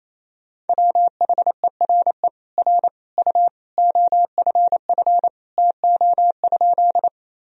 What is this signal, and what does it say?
Morse code, heard as a keyed tone, about 28 wpm, 700 Hz. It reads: WHERE R U OFF TO?
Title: Sentences from Top 200 Words and Abbreviations 28wpm